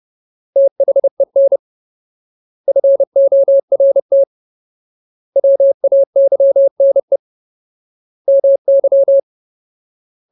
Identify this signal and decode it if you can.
Morse code, heard as a keyed tone, about 30 wpm, 550 Hz. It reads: THEN FORT WAYNE MY
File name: Ultimate Word List / 30WPM Ultimate